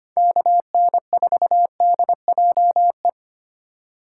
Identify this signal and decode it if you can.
Morse code, heard as a keyed tone, about 25 wpm, 700 Hz. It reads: KN4DJE
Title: USA Call Signs - 6 Characters - 2x Character Spacing - 25wpm